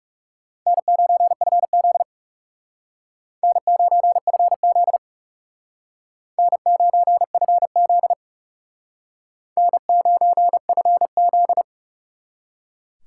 Morse code, heard as a keyed tone, about 40 wpm, 700 Hz. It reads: N9FZ N9FZ N9FZ N9FZ